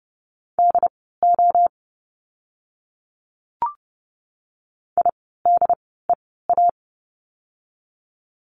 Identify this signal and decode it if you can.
Morse code, heard as a keyed tone, about 30 wpm, spaced as for 17 wpm, 700 Hz. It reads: DO IDEA
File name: course / 30:10wpm / 108 Lesson 18f - Letter m - Words 30wpm